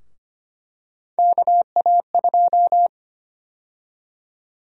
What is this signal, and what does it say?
Morse code, heard as a keyed tone, about 25 wpm, 700 Hz. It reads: KA2
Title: USA Call Sign Prefixes - 25wpm